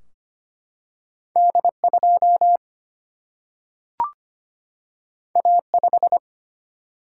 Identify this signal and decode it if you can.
Morse code, heard as a keyed tone, about 25 wpm, 700 Hz. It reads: D2 A5
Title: Single-Letter Single-Number 25wpm